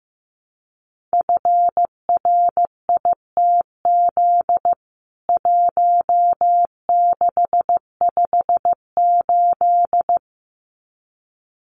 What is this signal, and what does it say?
Morse code, heard as a keyed tone, about 15 wpm, 700 Hz. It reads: FRITZ 1658
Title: CWT Exchange 15wpm